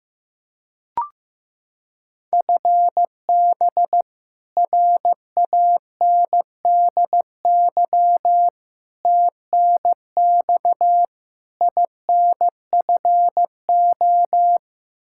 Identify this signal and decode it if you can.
Morse code, heard as a keyed tone, about 15 wpm, 700 Hz. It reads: FB RANDY TNX INFO
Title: K1USN SST QSO Practice 15wpm